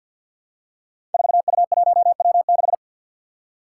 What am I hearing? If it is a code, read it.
Morse code, heard as a keyed tone, about 52 wpm, 700 Hz. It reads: 4U1W6
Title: Five Challenging Letter-Number Combinations 50wpm